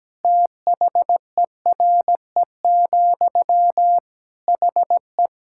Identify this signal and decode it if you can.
Morse code, heard as a keyed tone, about 17 wpm, 700 Hz. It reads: THERE, HE
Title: Sentences from Top 1600 Words 17wpm